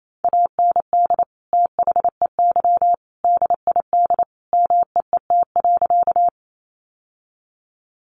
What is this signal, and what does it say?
Morse code, heard as a keyed tone, about 28 wpm, 700 Hz. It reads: AND THEY DID MEET.
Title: Sentences from Top 1100 Words 28wpm